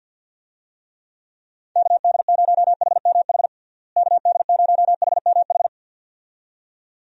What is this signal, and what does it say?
Morse code, heard as a keyed tone, about 46 wpm, 700 Hz. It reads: KD0HMH KD0HMH
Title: USA Call Signs - 6 Characters - Repeated 2x - 50wpm